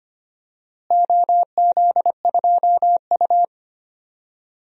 Morse code, heard as a keyed tone, about 25 wpm, 700 Hz. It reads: OZ2U
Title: DXCC-100 Call Signs - 4 Characters - Repeated 2x - 25wpm